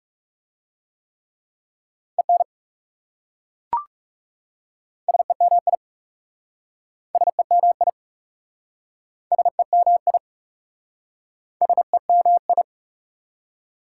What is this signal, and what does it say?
Morse code, heard as a keyed tone, about 40 wpm, 700 Hz. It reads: EN SEMI SEMI SEMI SEMI